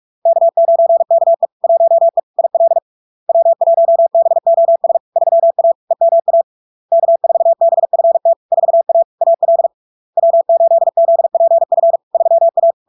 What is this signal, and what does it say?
Morse code, heard as a keyed tone, about 46 wpm, 670 Hz. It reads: K9KE 1E IL W1BOS 3A EMA K4BFT 4A AL W8ZPF 3A
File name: LARK-FD-2022-Log-in-CW-45wpm-670-Hz